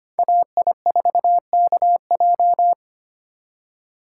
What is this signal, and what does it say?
Morse code, heard as a keyed tone, about 25 wpm, 700 Hz. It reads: AI4KJ
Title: USA Call Signs - 2x Character Spacing - 25wpm